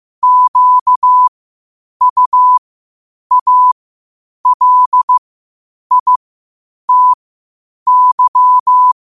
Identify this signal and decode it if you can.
Morse code, heard as a keyed tone, about 15 wpm, spaced as for 8 wpm, 1000 Hz. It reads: QUALITY
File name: qotdfiles / 8wpm